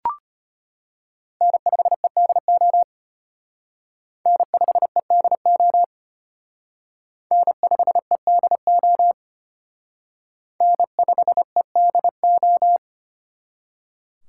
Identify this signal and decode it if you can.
Morse code, heard as a keyed tone, about 29 wpm, 700 Hz. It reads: N5EDO N5EDO N5EDO N5EDO
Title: Speed Racing - USA Call Signs 25wpm